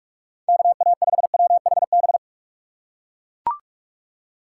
Morse code, heard as a keyed tone, about 43 wpm, 700 Hz. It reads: KA5WHB